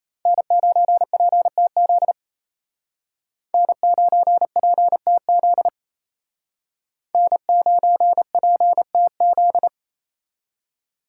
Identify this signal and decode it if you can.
Morse code, heard as a keyed tone, about 32 wpm, 700 Hz. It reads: N9PTZ N9PTZ N9PTZ
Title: Speed Racing - USA 1x3 Call Signs 28wpm